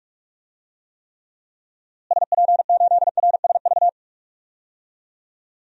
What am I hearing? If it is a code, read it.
Morse code, heard as a keyed tone, about 46 wpm, 700 Hz. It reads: SP8RSV
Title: DXCC-100 Call Signs - 6 Characters - Repeated 2x - 45wpm